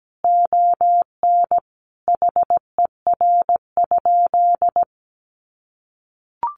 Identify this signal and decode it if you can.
Morse code, heard as a keyed tone, about 17 wpm, 700 Hz. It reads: ON HER?
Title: Sentences from Top 1200 Words 17wpm